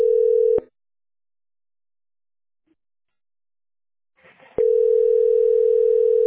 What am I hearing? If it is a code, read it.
Ringback tone (North America), 440 + 480 Hz.